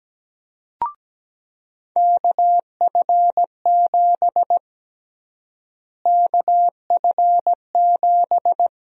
Morse code, heard as a keyed tone, about 17 wpm, 700 Hz. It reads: KF7 KF7
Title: USA Call Sign Prefixes - Repeated 2x - 17wpm